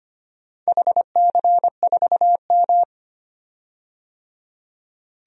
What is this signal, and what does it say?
Morse code, heard as a keyed tone, about 25 wpm, 700 Hz. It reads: HC4M
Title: DXCC-100 Call Signs - 4 Characters - 25wpm